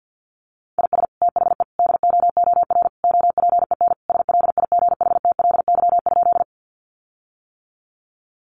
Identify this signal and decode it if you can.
Morse code, heard as a keyed tone, about 50 wpm, 700 Hz. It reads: IS THE DOOR OPEN SLIGHTLY?
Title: Sentences from Top 1300 Words 50wpm